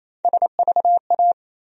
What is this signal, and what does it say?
Morse code, heard as a keyed tone, about 28 wpm, 700 Hz. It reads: SVA
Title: USA Call Sign Postfixes - 3 Letters 28wpm